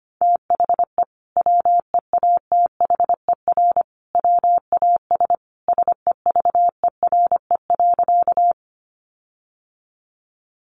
Morse code, heard as a keyed tone, about 25 wpm, 700 Hz. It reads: THE WEATHER WAS SEVERE.
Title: Sentences from Top 1400 Words 25wpm